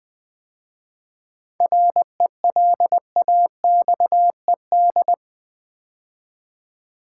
Morse code, heard as a keyed tone, about 20 wpm, 700 Hz. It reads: RELAXED
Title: 7 Letter Words - First 3 Letters Given - 20wpm